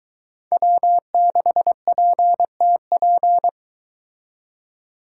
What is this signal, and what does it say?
Morse code, heard as a keyed tone, about 22 wpm, 700 Hz. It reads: W6PTP